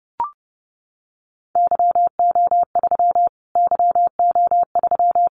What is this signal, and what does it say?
Morse code, heard as a keyed tone, about 30 wpm, 700 Hz. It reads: YO3 YO3
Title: Callsign Prefixes for Top 25 DXCC - Repeated 3x - 30wpm